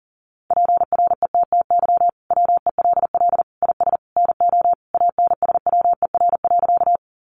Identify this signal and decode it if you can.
Morse code, heard as a keyed tone, about 40 wpm, 700 Hz. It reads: PRETTY WELL IS NO ANSWER.
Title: Sentences from Top 1000 Words 40wpm